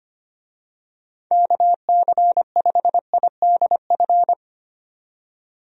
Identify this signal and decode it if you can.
Morse code, heard as a keyed tone, about 25 wpm, 700 Hz. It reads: KC5IDF
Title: USA Call Signs - 6 Characters - 25wpm